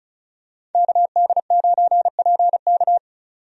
Morse code, heard as a keyed tone, about 36 wpm, 700 Hz. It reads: KD9PK
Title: USA Call Signs - 5 Characters - 35wpm